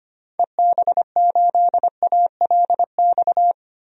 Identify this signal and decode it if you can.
Morse code, heard as a keyed tone, about 25 wpm, 700 Hz. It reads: EB8ALX
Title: DXCC-100 Call Signs - 6 Characters - 25wpm